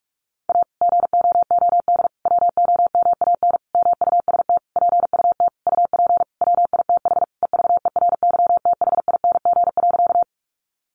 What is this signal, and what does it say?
Morse code, heard as a keyed tone, about 46 wpm, 700 Hz. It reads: A GOOD WOMAN MUST PUT UP WITH EVERYTHING.
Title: Sentences from Top 1100 Words 45wpm